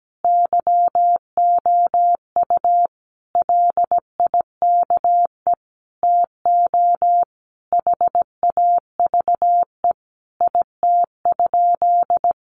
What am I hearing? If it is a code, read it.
Morse code, heard as a keyed tone, about 17 wpm, 700 Hz. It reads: YOU LIKE TO HAVE IT?
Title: Sentences from Top 100 Words 17wpm